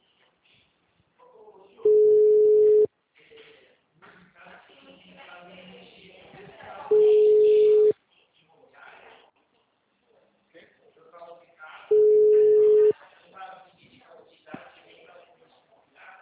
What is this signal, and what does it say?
Ringback tone (Europe, 425 Hz), with speech.
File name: out- -2003- - - .5381